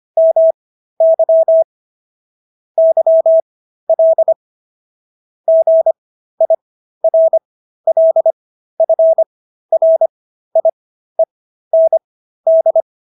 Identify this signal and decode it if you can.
Morse code, heard as a keyed tone, about 25 wpm, spaced as for 13 wpm, 650 Hz. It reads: MY YL GIRLFRIEND